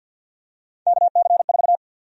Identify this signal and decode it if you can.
Morse code, heard as a keyed tone, about 48 wpm, 700 Hz. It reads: KC4
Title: USA Call Sign Prefixes - 50wpm